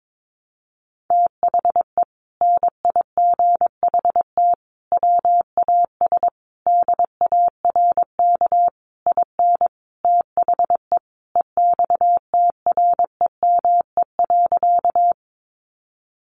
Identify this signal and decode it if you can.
Morse code, heard as a keyed tone, about 22 wpm, 700 Hz. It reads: THE NIGHT WAS DARK IN THE EXTREME.
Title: Sentences from Top 1300 Words 22wpm